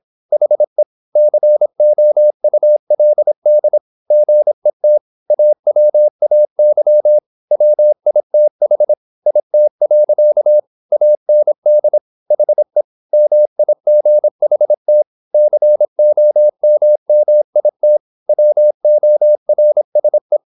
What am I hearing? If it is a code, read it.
Morse code, heard as a keyed tone, about 26 wpm, 600 Hz. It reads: HE COULD GET AWAY WITH IT. AND HE MIGHT COMMIT WORSE
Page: Never 26wpm | CW med Gnister